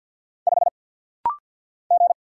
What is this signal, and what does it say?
Morse code, heard as a keyed tone, about 47 wpm, 700 Hz. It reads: 5 G
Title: Single Letter-Number Alt - Rapid-Fire 50wpm